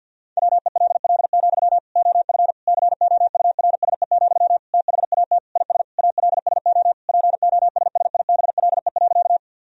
Morse code, heard as a keyed tone, about 51 wpm, 700 Hz. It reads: WELL, OF COURSE, THAT IS ALSO POSSIBLE.
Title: Sentences from Top 1000 Words - Review Part 1 - 50wpm